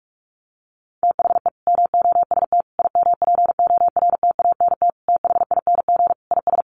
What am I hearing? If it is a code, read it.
Morse code, heard as a keyed tone, about 44 wpm, 700 Hz. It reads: THE MOST IMPORTANT THING IS
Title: Sets of 5 Words Encore 45wpm